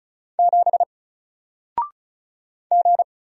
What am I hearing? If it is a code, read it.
Morse code, heard as a keyed tone, about 36 wpm, 700 Hz. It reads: 7 G
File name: Single-Letter-Number-35wpm